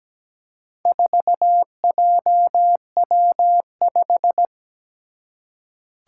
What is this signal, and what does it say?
Morse code, heard as a keyed tone, about 17 wpm, 700 Hz. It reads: 4JW5
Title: Four Challenging Letter-Number Combinations 17wpm